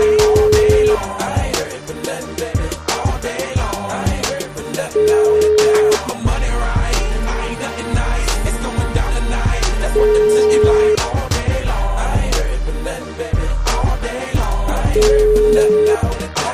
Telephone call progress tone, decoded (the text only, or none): ringback tone